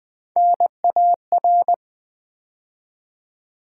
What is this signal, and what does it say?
Morse code, heard as a keyed tone, about 20 wpm, 700 Hz. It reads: NAR
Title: Build A Word - 3 Letters At A Time - 9-Letter Words - 20wpm